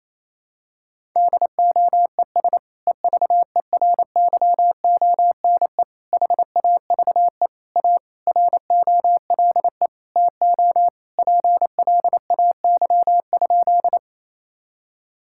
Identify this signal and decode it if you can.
Morse code, heard as a keyed tone, about 28 wpm, 700 Hz. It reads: DOES EVERYONE HAVE A ROLE TO PLAY?
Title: Sentences from Top 1000 Words - Review Part 6 - 28wpm